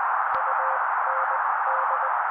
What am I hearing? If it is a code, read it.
Morse code, heard as a keyed tone, about 20 wpm, spaced as for 16 wpm, 530 Hz. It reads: UND
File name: daily / 16wpm noise